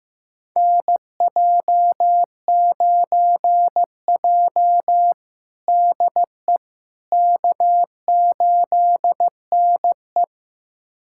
Morse code, heard as a keyed tone, about 16 wpm, 700 Hz. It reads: NJ9J DE K8NE
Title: USA Callsign DE Callsign - 4 Characters - 15wpm